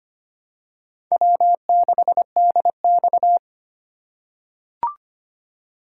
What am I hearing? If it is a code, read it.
Morse code, heard as a keyed tone, about 25 wpm, 700 Hz. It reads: W6DX
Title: USA Call Signs - 4 Characters - 2x Character Spacing - 25wpm